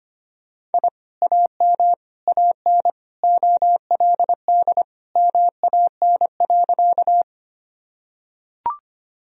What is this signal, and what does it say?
Morse code, heard as a keyed tone, about 25 wpm, 700 Hz. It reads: I AM AN OLD MAN.
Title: Sentences from Top 1000 Words - Review Part 2 - 25wpm